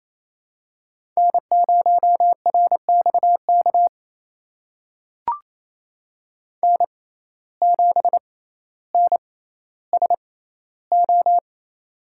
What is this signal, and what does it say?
Morse code, heard as a keyed tone, about 29 wpm, spaced as for 15 wpm, 700 Hz. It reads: N0RXK N7NSO
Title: USA Call Signs - 5 Characters - 6x Character Spacing - 28wpm